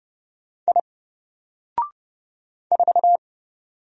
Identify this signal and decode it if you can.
Morse code, heard as a keyed tone, about 30 wpm, 700 Hz. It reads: I 4
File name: Single-Letter-Number-30wpm